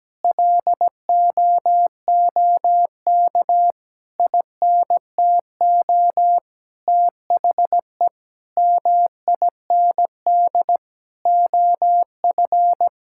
Morse code, heard as a keyed tone, about 17 wpm, 700 Hz. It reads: LOOK INTO THE MIND OF
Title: Sentences from Top 1600 Words 17wpm